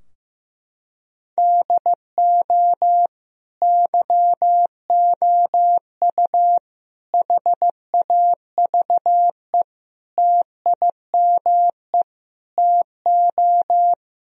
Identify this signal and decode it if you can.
Morse code, heard as a keyed tone, about 15 wpm, 700 Hz. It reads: DO YOU HAVE TIME TO
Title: Sentences from Top 1000 Words - Review Part 4 - 15wpm